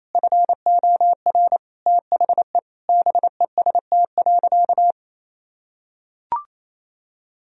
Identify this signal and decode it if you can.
Morse code, heard as a keyed tone, about 28 wpm, 700 Hz. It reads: FOR THE BEST.
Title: Sentences from Top 1000 Words - Review Part 3 - 28wpm